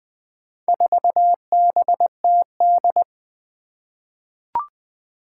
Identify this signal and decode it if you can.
Morse code, heard as a keyed tone, about 21 wpm, 700 Hz. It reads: 4BTD
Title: Four Challenging Letter-Number Combinations 20wpm